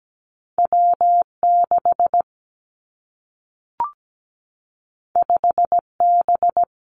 Morse code, heard as a keyed tone, about 17 wpm, 700 Hz. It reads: W 6 5 B
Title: Two Challenging Letter-Number Combinations 17wpm